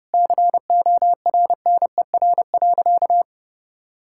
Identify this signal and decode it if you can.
Morse code, heard as a keyed tone, about 30 wpm, 700 Hz. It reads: CORNER.
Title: Sentences from Top 1600 Words 30wpm